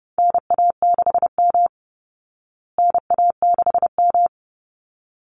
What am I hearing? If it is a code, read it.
Morse code, heard as a keyed tone, about 30 wpm, 700 Hz. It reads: NA6M NA6M
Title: Call Signs Repeated 3x - 30wpm